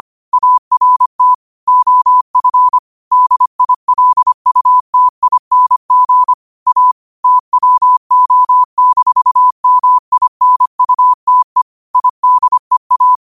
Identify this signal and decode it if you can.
Morse code, heard as a keyed tone, about 25 wpm, 1000 Hz. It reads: ART OF DILUTING A TWO-MINUTE IDEA
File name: qotdfiles / 25wpm